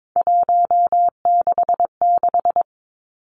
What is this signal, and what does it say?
Morse code, heard as a keyed tone, about 22 wpm, 700 Hz. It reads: 166
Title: Three Challenging Letter-Number Combinations 22wpm